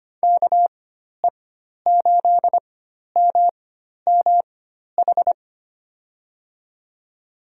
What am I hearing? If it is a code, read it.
Morse code, heard as a keyed tone, about 25 wpm, spaced as for 12 wpm, 700 Hz. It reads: KE8MMH
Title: USA Call Signs - 6 Characters - 4x Character Spacing - 25wpm